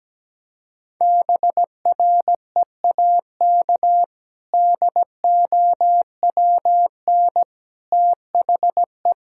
Morse code, heard as a keyed tone, about 17 wpm, 700 Hz. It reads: BREAK DOWN THE
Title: Sentences from Top 1000 Words - Review Part 6 - 17wpm